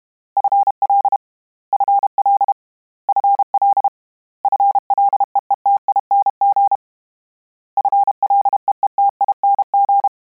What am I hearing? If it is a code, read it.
Morse code, heard as a keyed tone, about 32 wpm, 800 Hz. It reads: FL FL FL FLEETING FLEETING
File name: Combinations / Combinations-mp3s / Combinations@30